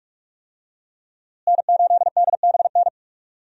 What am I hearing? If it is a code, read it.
Morse code, heard as a keyed tone, about 46 wpm, 700 Hz. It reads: N8DBN
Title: USA Call Signs - 5 Characters - Repeated 2x - 45wpm